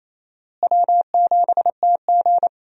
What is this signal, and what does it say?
Morse code, heard as a keyed tone, about 28 wpm, 700 Hz. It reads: W7TG
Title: USA Call Signs - 4 Characters - Repeated 2x - 28wpm